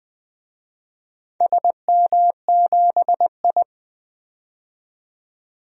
Morse code, heard as a keyed tone, about 20 wpm, 700 Hz. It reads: SM7I
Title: DXCC-100 Call Signs - 4 Characters - 20wpm